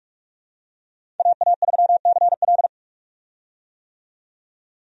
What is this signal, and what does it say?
Morse code, heard as a keyed tone, about 44 wpm, 700 Hz. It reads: AA3CL
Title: USA Call Signs - 5 Characters - 45wpm